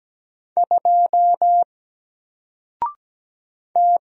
Morse code, heard as a keyed tone, about 17 wpm, 700 Hz. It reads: 2 T
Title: Single Letter-Number 17wpm